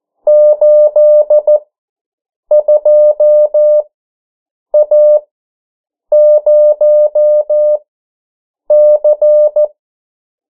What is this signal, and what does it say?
Morse code, heard as a keyed tone, about 14 wpm, spaced as for 7 wpm, 600 Hz. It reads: 82A0C
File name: morse mp3 / 09 urok 8